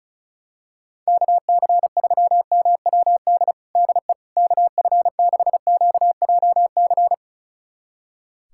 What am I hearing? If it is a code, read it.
Morse code, heard as a keyed tone, about 36 wpm, 700 Hz. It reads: KC3MWD DE KF6QJC
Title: USA Callsign DE Callsign 35wpm